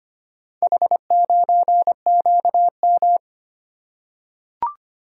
Morse code, heard as a keyed tone, about 25 wpm, 700 Hz. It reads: H9QM
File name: practice / 25wpm / Four Letter-Number Combinations - 25wpm